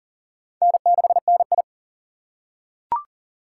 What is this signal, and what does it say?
Morse code, heard as a keyed tone, about 38 wpm, 700 Hz. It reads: N6NI
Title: USA Call Signs - 4 Characters - 2x Character Spacing - 40wpm